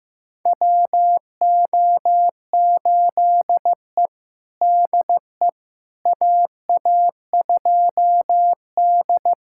Morse code, heard as a keyed tone, about 15 wpm, 700 Hz. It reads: WO8E DE AA2D
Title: USA Callsign DE Callsign - 4 Characters - 15wpm